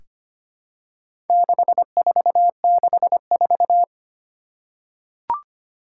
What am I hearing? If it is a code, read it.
Morse code, heard as a keyed tone, about 25 wpm, 700 Hz. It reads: 6464
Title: Four Challenging Letter-Number Combinations 25wpm